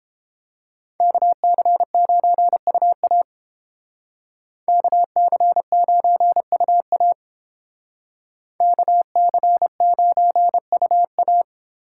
Morse code, heard as a keyed tone, about 29 wpm, 700 Hz. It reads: KC9UA KC9UA KC9UA